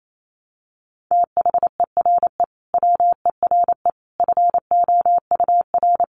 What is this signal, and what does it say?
Morse code, heard as a keyed tone, about 28 wpm, 700 Hz. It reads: THERE WERE FOUR
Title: A Sound Of Thunder 28wpm